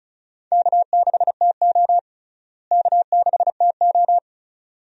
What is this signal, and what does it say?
Morse code, heard as a keyed tone, about 34 wpm, 700 Hz. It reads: K6TO K6TO
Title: USA Call Signs - 4 Characters - Repeated 2x - 35wpm